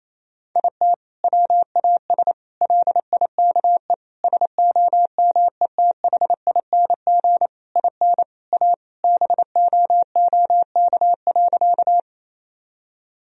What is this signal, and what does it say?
Morse code, heard as a keyed tone, about 28 wpm, 700 Hz. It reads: IT WAS LIKE SOMETHING IN A BOOK.
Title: Sentences from Top 1000 Words - Review Part 1 - 28wpm